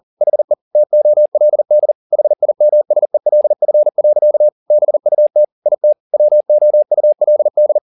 Morse code, heard as a keyed tone, about 40 wpm, 600 Hz. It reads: HE TOLD HIMSELF. BUT IT WOULD
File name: the old man and the sea 2 40wpm